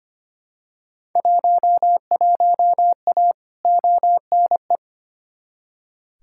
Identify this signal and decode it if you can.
Morse code, heard as a keyed tone, about 25 wpm, 700 Hz. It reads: 11A ONE
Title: Field Day Exchanges Encore - 25wpm